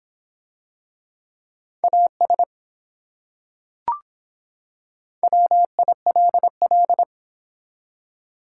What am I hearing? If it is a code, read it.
Morse code, heard as a keyed tone, about 24 wpm, 700 Hz. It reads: AS WILL